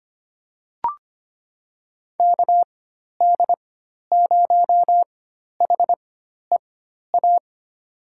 Morse code, heard as a keyed tone, about 25 wpm, spaced as for 12 wpm, 700 Hz. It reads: KD0HEA
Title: USA Call Signs - 6 Characters - 4x Character Spacing - 25wpm